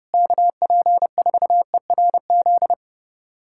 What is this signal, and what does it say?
Morse code, heard as a keyed tone, about 30 wpm, 700 Hz. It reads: KP4ERZ
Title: USA Callsign DE Callsign - 6 Characters - 30wpm